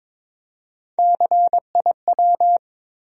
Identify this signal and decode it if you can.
Morse code, heard as a keyed tone, about 22 wpm, 700 Hz. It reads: CIW
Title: USA Call Sign Postfixes - 3 Letters 22wpm